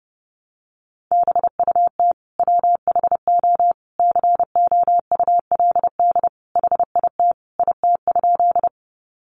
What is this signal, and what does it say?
Morse code, heard as a keyed tone, about 30 wpm, 700 Hz. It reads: BUT WHO COULD HIT IT?
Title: Sentences from Top 1200 Words 30wpm